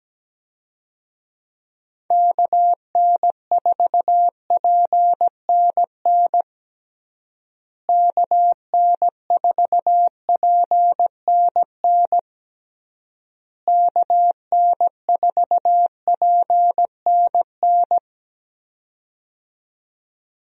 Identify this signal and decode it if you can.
Morse code, heard as a keyed tone, about 17 wpm, 700 Hz. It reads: KN4PNN KN4PNN KN4PNN
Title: USA Call Signs - 6 Characters - Repeated 3x - 17wpm